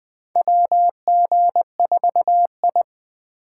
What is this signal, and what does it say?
Morse code, heard as a keyed tone, about 20 wpm, 700 Hz. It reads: WG4I
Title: 2023 SOTA Chaser Call Signs - 20wpm